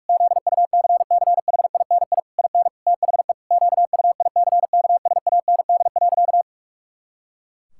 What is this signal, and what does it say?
Morse code, heard as a keyed tone, about 41 wpm, 700 Hz. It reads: ZUCCHINI IN THE QUICKSAND.
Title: Pangram Challenge v2 - 45wpm